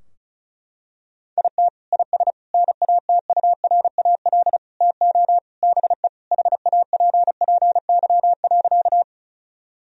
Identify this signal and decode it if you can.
Morse code, heard as a keyed tone, about 36 wpm, 700 Hz. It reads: IT IS NATURAL TO BE HAPPY.
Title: Sentences from Top 1000 Words - Review Part 4 - 35wpm